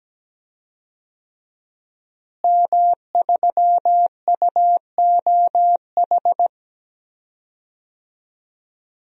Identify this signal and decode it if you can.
Morse code, heard as a keyed tone, about 17 wpm, 700 Hz. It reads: M3UOH
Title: DXCC-100 Call Signs - 5 Characters - Repeated 2x - 17wpm